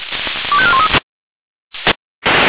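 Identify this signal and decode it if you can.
Five-tone selective call (ZVEI), heard as quick runs of five tones, tones of about 70 ms. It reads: 25413
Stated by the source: ZVEI five-tone selective call received on an SDR.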